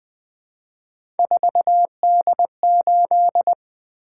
Morse code, heard as a keyed tone, about 20 wpm, 700 Hz. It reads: 4D8
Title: Three Letter-Number Combinations - 20wpm